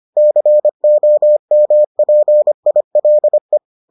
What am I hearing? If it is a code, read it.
Morse code, heard as a keyed tone, about 25 wpm, 600 Hz. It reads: COMPILE